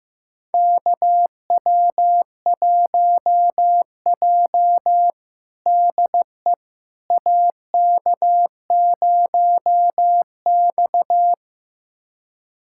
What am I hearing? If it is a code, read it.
Morse code, heard as a keyed tone, about 15 wpm, 700 Hz. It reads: KW1J DE AK0X
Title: USA Callsign DE Callsign - 4 Characters - 15wpm